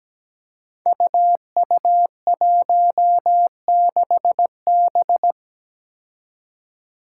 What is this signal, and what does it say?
Morse code, heard as a keyed tone, about 17 wpm, 700 Hz. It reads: UU16B
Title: Five Challenging Letter-Number Combinations 17wpm